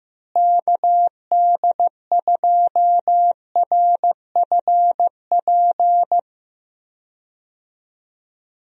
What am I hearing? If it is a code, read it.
Morse code, heard as a keyed tone, about 15 wpm, 700 Hz. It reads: KD2RFP
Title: USA Call Signs - 6 Characters - Repeated 3x - 15wpm